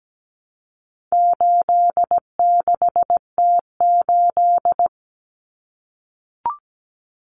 Morse code, heard as a keyed tone, about 17 wpm, 700 Hz. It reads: 86T8
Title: Common Cut Numbers - 4-Digit 17wpm